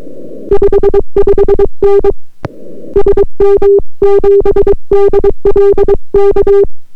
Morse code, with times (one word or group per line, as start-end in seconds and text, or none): 0.5-2.1 55N
3.0-6.6 SM7DLK